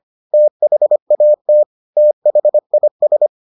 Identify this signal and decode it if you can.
Morse code, heard as a keyed tone, about 25 wpm, 600 Hz. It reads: THAT THIS